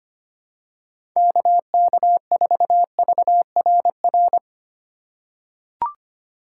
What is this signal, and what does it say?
Morse code, heard as a keyed tone, about 25 wpm, 700 Hz. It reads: KK4VRR
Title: USA Call Signs - 6 Characters - 4x Character Spacing - 25wpm